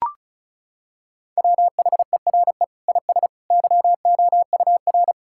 Morse code, heard as a keyed tone, about 34 wpm, 700 Hz. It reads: WHERE IS YOUR
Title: Sentences from Top 1000 Words - Review Part 5 - 35wpm